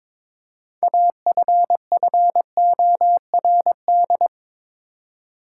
Morse code, heard as a keyed tone, about 22 wpm, 700 Hz. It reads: AFFORD
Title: Six Letter Words - 22wpm